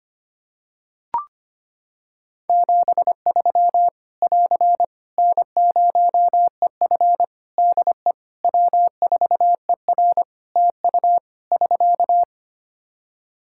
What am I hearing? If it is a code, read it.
Morse code, heard as a keyed tone, about 25 wpm, 700 Hz. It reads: 73 <AR> N0EF DE W4ER TU <SK>
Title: Practice QSO - Exchange 4 - Ending - 25wpm